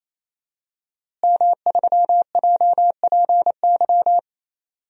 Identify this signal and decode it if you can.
Morse code, heard as a keyed tone, about 28 wpm, 700 Hz. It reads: M3JPY
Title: DXCC-100 Call Signs - 5 Characters - Repeated 3x - 28wpm